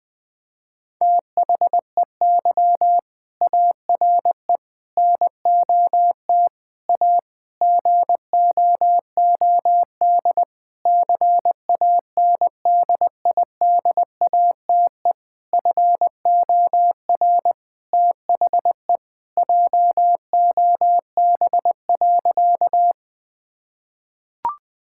Morse code, heard as a keyed tone, about 20 wpm, 700 Hz. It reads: THEY ARE NOT A GOOD CANDIDATE FOR THE JOB.
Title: Sentences from Top 1600 Words 20wpm